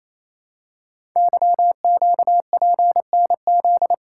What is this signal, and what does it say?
Morse code, heard as a keyed tone, about 29 wpm, 700 Hz. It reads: YQPNZ
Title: Code Groups - 5 Letters - 28wpm